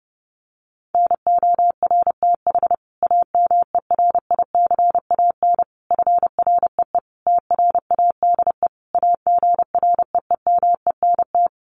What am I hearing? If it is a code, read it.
Morse code, heard as a keyed tone, about 30 wpm, 700 Hz. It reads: NORTH AMERICAN FREE TRADE AGREEMENT
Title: Sets of 5 Words Encore 30wpm